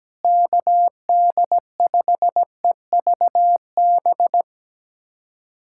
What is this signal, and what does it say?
Morse code, heard as a keyed tone, about 17 wpm, 700 Hz. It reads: KD5EVB
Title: USA Callsign DE Callsign - 6 Characters - 17wpm